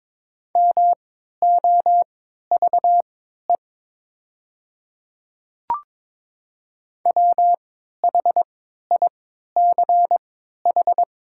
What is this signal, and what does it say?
Morse code, heard as a keyed tone, about 22 wpm, spaced as for 12 wpm, 700 Hz. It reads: MOVE WHICH
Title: 3x Character Spacing - Top 300 Words - 22wpm